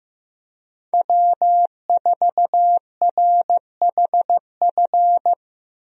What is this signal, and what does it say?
Morse code, heard as a keyed tone, about 15 wpm, 700 Hz. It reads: W4RHF
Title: USA Call Signs - 5 Characters - Repeated 3x - 15wpm